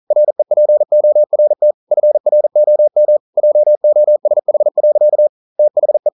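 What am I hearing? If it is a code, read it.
Morse code, heard as a keyed tone, about 42 wpm, 600 Hz. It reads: REPORT FROM JOSH. THE
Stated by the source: Morse code practice at 41 wpm, from CW med Gnister.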